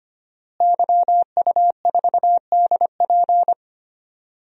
Morse code, heard as a keyed tone, about 25 wpm, 700 Hz. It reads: YU4DP
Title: DXCC-100 Call Signs - 5 Characters - Repeated 2x - 25wpm